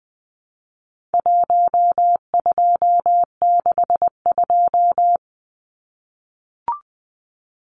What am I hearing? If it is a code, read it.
Morse code, heard as a keyed tone, about 20 wpm, 700 Hz. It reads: 1262
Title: Numbers - 4-Digit 20wpm